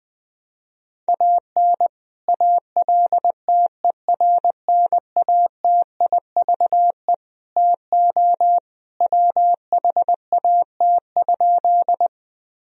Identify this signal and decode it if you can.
Morse code, heard as a keyed tone, about 21 wpm, 700 Hz. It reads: AN ALTERNATIVE TO WHAT?
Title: Sentences from Top 1000 Words - Review Part 4 - 20wpm